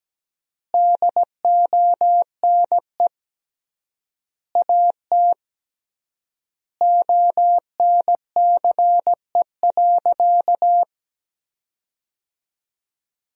Morse code, heard as a keyed tone, about 17 wpm, 700 Hz. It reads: DONE AT ONCE.
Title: 3x Word-Spacing - Sentences from Top 500 Words - 17wpm